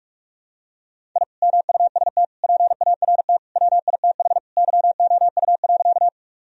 Morse code, heard as a keyed tone, about 41 wpm, 700 Hz. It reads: I MUST PART WITH YOU.
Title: Sentences from Top 1000 Words - Review Part 1 - 45wpm